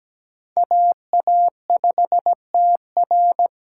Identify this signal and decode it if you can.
Morse code, heard as a keyed tone, about 17 wpm, 700 Hz. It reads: AA5TR
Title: USA Call Signs - 5 Characters - 2x Character Spacing - 17wpm